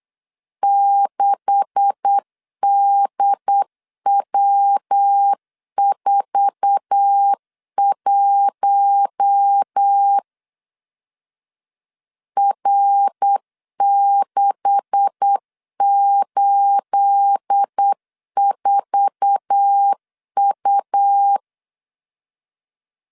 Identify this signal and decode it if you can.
Morse code, heard as a keyed tone, about 8 wpm, 800 Hz. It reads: 6DW41 R684U
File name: LEZIONI / 10 SIGNS / 10A SLASH 8wpm 3452